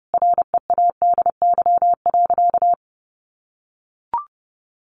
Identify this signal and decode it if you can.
Morse code, heard as a keyed tone, about 30 wpm, 700 Hz. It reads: READY.
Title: Sentences from Top 1000 Words 30wpm